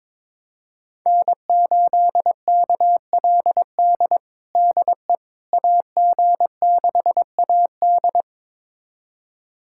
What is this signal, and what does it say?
Morse code, heard as a keyed tone, about 22 wpm, 700 Hz. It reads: N8KLD DE AG6AD
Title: USA Callsign DE Callsign - 5 Characters - 22wpm